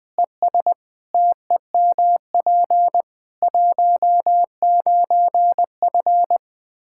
Morse code, heard as a keyed tone, about 19 wpm, 700 Hz. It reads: ES TEMP 19F
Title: Practice QSO - Exchange 2 - Info - 20wpm